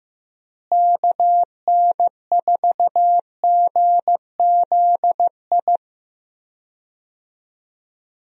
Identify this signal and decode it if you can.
Morse code, heard as a keyed tone, about 15 wpm, 700 Hz. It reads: KN4GZI
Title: USA Call Signs - 6 Characters - Repeated 3x - 15wpm